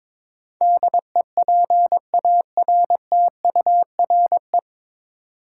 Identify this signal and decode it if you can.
Morse code, heard as a keyed tone, about 21 wpm, 700 Hz. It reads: DEPARTURE
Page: Morse Code Ninja practice 22wpm